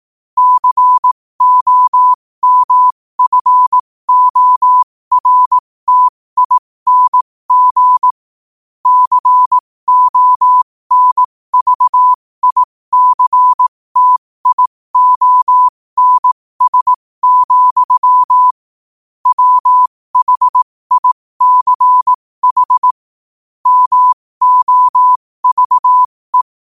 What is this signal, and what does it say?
Morse code, heard as a keyed tone, about 18 wpm, spaced as for 15 wpm, 1000 Hz. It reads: COMFORTING CONVICTIONS, WHICH MOVE
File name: qotdfiles / 15wpm